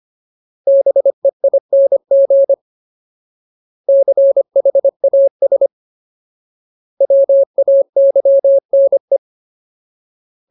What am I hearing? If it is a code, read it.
Morse code, heard as a keyed tone, about 25 wpm, 550 Hz. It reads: BEING CHAS WAYNE
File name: Ultimate Word List / 25WPM Ultimate